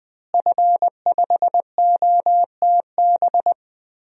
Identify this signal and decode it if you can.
Morse code, heard as a keyed tone, about 20 wpm, 700 Hz. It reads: F5OTB
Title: DXCC-100 Call Signs - 5 Characters - Repeated 3x - 20wpm